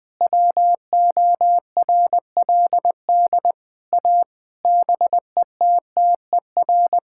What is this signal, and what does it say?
Morse code, heard as a keyed tone, about 20 wpm, 700 Hz. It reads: WORLD A BETTER
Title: Sentences with 7-Word Phrases - Spoken Beginning - 20wpm